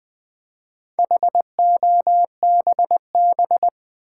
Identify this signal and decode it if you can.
Morse code, heard as a keyed tone, about 20 wpm, 700 Hz. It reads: HOBB
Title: Build A Word - 1 Letter At A Time - 5-Letter Words - 20wpm